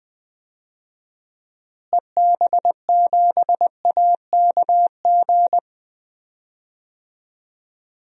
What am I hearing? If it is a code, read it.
Morse code, heard as a keyed tone, about 20 wpm, 700 Hz. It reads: EB7AKG
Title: DXCC-100 Call Signs - 6 Characters - Repeated 3x - 20wpm